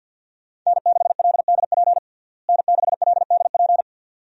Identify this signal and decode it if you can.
Morse code, heard as a keyed tone, about 49 wpm, 700 Hz. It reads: N6LDP N6LDP
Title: USA Call Signs - 5 Characters - Repeated 2x - 50wpm